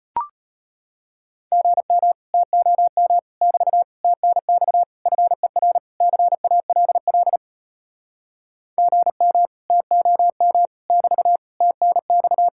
Morse code, heard as a keyed tone, about 34 wpm, 700 Hz. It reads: GM TOM <BT> TNX FER CALL GM TOM <BT> TNX